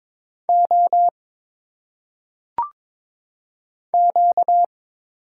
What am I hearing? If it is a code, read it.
Morse code, heard as a keyed tone, about 22 wpm, 700 Hz. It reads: O Q
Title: Single Letters - Phonetic Pronunciation 22wpm